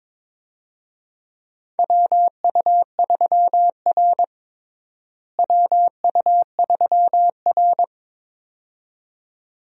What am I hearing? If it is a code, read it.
Morse code, heard as a keyed tone, about 22 wpm, 700 Hz. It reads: WU3R WU3R
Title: USA Call Signs - 4 Characters - Repeated 2x - 22wpm